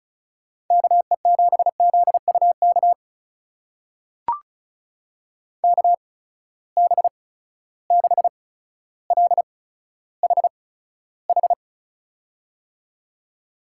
Morse code, heard as a keyed tone, about 34 wpm, spaced as for 15 wpm, 700 Hz. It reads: KE7ZUK KB6LHH
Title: USA Call Signs - 6 Characters - 8x Character Spacing - 35wpm